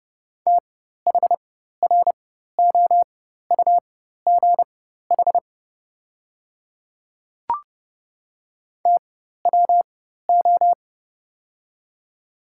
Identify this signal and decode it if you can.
Morse code, heard as a keyed tone, about 31 wpm, spaced as for 14 wpm, 700 Hz. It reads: THROUGH TWO
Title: 4x Character Spacing - Top 200 Words - 30wpm